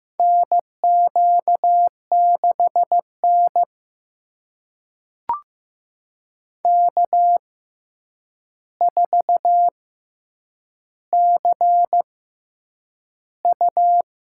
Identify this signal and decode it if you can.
Morse code, heard as a keyed tone, about 15 wpm, spaced as for 8 wpm, 700 Hz. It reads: NQ6N K4CU
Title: USA Call Signs - 4 Characters - 6x Character Spacing - 15wpm